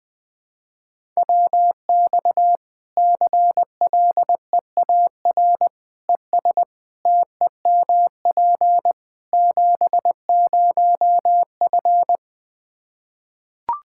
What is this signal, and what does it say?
Morse code, heard as a keyed tone, about 21 wpm, 700 Hz. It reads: WX CLEAR ES TEMP 70F
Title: Practice QSO - Exchange 2 - Info - 20wpm